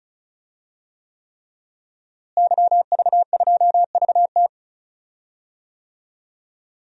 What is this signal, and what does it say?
Morse code, heard as a keyed tone, about 36 wpm, 700 Hz. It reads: YV2VT
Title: DXCC-100 Call Signs - 5 Characters - Repeated 3x - 35wpm